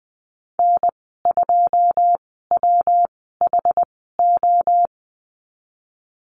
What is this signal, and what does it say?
Morse code, heard as a keyed tone, about 20 wpm, spaced as for 15 wpm, 700 Hz. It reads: N2WHO
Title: 128 Lesson 21g - Number 6 - Call Signs 20wpm